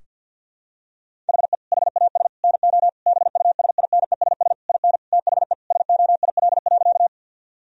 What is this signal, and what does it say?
Morse code, heard as a keyed tone, about 50 wpm, 700 Hz. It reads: HE HAS NO BUSINESS IN THE SOIL.
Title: Sentences from Top 1000 Words - Review Part 4 - 50wpm